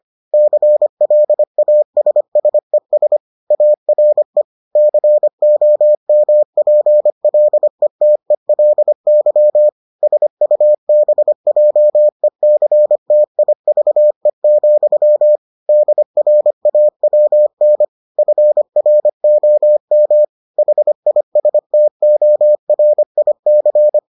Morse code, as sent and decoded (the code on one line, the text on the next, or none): -.-. .-.. .- ... ... ... / .- .-. / -.-. --- -- .--. .-.. - . .-.. -.-- / ... ..- -... .--- . -.-. - .. ...- . --..-- / -.. .-. .- .-- -. / ..-. .-. --- -- / .... .. ... - --- .-. .. -.-.
CLASSES ARE COMPLETELY SUBJECTIVE, DRAWN FROM HISTORIC